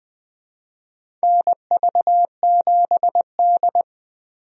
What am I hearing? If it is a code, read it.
Morse code, heard as a keyed tone, about 20 wpm, 700 Hz. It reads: NV7D
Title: USA Call Signs - 4 Characters - 20wpm